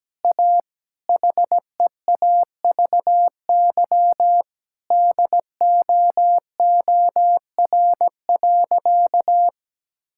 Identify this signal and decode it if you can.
Morse code, heard as a keyed tone, about 17 wpm, 700 Hz. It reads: A HEAVY DOOR.
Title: Sentences from Top 1600 Words 17wpm